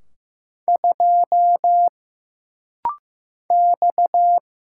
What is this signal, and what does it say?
Morse code, heard as a keyed tone, about 15 wpm, 700 Hz. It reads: 2 X
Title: Single Letter-Number Alt - Rapid-Fire 15wpm